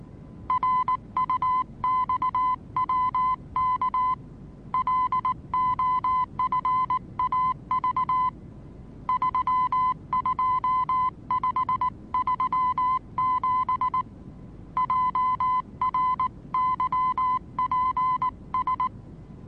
Morse code, heard as a keyed tone, about 19 wpm, 1020 Hz. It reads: RUXWK LOFAV 32537 JRYPS
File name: repos / Morsekurs / DJ4UF Morsen26